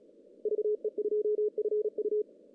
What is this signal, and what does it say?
Morse code, heard as a keyed tone, about 37 wpm, 370 Hz. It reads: VE2FU
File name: 2015 cq ww cw / 2015 cqwww k5tr - 0-41